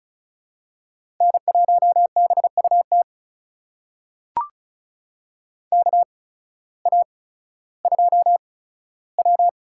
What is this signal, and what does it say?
Morse code, heard as a keyed tone, about 34 wpm, spaced as for 16 wpm, 700 Hz. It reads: N1BUT KA2W
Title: USA Call Signs - 8x Character Spacing - 35wpm